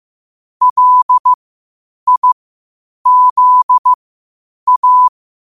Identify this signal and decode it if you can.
Morse code, heard as a keyed tone, about 15 wpm, spaced as for 8 wpm, 1000 Hz. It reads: LIZA